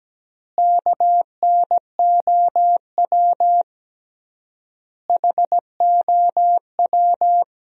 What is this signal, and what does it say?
Morse code, heard as a keyed tone, about 17 wpm, 700 Hz. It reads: KNOW HOW
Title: 3x Word-Spacing - Sentences from Top 100 Words - 17wpm